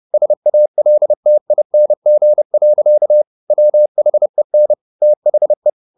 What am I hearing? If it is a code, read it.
Morse code, heard as a keyed tone, about 30 wpm, 600 Hz. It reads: SALTING. WHEN THE
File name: the old man and the sea / the old man and the sea 1 30wpm